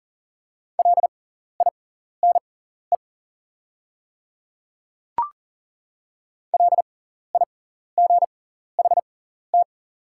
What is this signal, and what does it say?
Morse code, heard as a keyed tone, about 40 wpm, spaced as for 14 wpm, 700 Hz. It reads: LINE LIGHT
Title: 6x Character Spacing - Top 500 Words - 40wpm